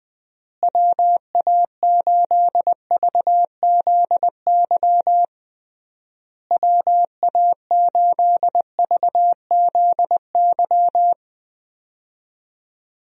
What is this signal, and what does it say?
Morse code, heard as a keyed tone, about 20 wpm, 700 Hz. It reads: WA8VZY WA8VZY
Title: USA Call Signs - 6 Characters - Repeated 3x - 20wpm